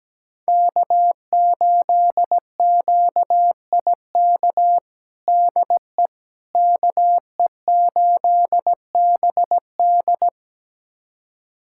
Morse code, heard as a keyed tone, about 17 wpm, 700 Hz. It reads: K8QIK DE KE8BD
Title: USA Callsign DE Callsign - 5 Characters - 17wpm